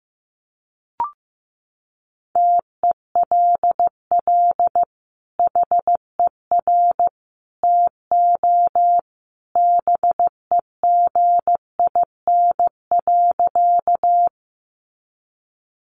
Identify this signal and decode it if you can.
Morse code, heard as a keyed tone, about 15 wpm, 700 Hz. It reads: TELL HER TO BEGIN.
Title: Sentences from Top 300 Words 15wpm